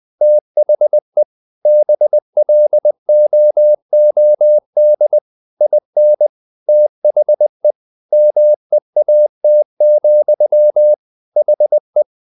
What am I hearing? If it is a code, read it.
Morse code, heard as a keyed tone, about 20 wpm, 600 Hz. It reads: THE BLOOD IN THE MEAT, HE
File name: the old man and the sea part 2 20wpm